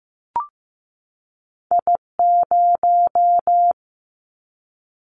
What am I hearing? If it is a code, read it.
Morse code, heard as a keyed tone, about 15 wpm, 700 Hz. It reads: I0
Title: Callsign Prefixes for Top 25 DXCC - 15wpm